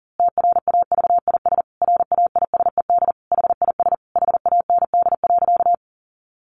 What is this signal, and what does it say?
Morse code, heard as a keyed tone, about 37 wpm, 700 Hz. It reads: TRAVIS RAISED HIS HAND.
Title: A Sound Of Thunder 40wpm